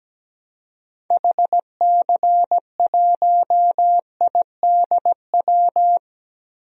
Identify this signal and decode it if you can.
Morse code, heard as a keyed tone, about 17 wpm, 700 Hz. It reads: HC1IDW
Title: DXCC-100 Call Signs - 6 Characters - Repeated 3x - 17wpm